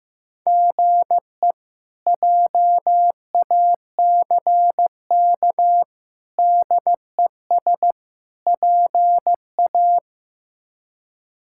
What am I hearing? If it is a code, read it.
Morse code, heard as a keyed tone, about 15 wpm, 700 Hz. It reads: GE JACK DES PA
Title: K1USN SST QSO Practice - 15wpm